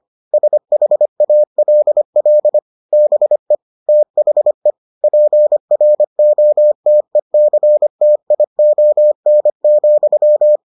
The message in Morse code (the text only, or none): SHALL BE THE PROTECTION,